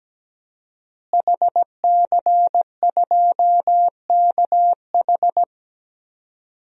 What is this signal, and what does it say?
Morse code, heard as a keyed tone, about 17 wpm, 700 Hz. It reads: HC2KH
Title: DXCC-100 Call Signs - 5 Characters - Repeated 2x - 17wpm